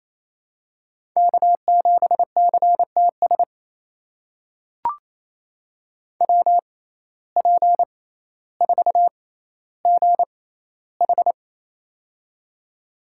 Morse code, heard as a keyed tone, about 28 wpm, spaced as for 14 wpm, 700 Hz. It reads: K7CTS WP4GH
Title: USA Call Signs - 5 Characters - 6x Character Spacing - 28wpm